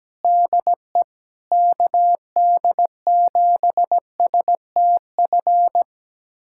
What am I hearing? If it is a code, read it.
Morse code, heard as a keyed tone, about 17 wpm, 700 Hz. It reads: DE KD7STF
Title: USA Callsign DE Callsign 17wpm